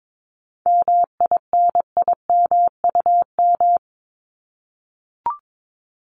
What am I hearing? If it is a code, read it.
Morse code, heard as a keyed tone, about 22 wpm, 700 Hz. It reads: MINIMUM
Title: Top 1251-1500 Words 22wpm